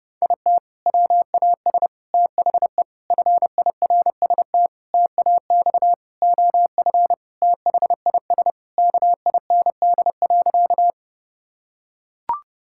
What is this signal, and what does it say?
Morse code, heard as a keyed tone, about 32 wpm, 700 Hz. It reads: IT WAS THE FIRST TAX OF THIS KIND.
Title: Sentences from Top 1000 Words - Review Part 5 - 30wpm